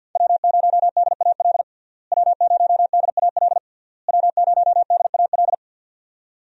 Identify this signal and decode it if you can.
Morse code, heard as a keyed tone, about 51 wpm, 700 Hz. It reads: W0DAL W0DAL W0DAL
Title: USA Call Signs - 5 Characters - Repeated 3x - 50wpm